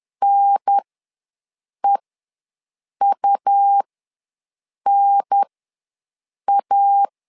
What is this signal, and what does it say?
Morse code, heard as a keyed tone, about 10 wpm, spaced as for 6 wpm, 800 Hz. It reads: NEUNA
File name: 2 UHDN ws 10wpm